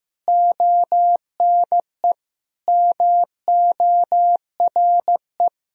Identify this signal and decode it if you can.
Morse code, heard as a keyed tone, about 15 wpm, 700 Hz. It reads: ONE MORE
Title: Sentences from Top 1000 Words - Review Part 2 - 15wpm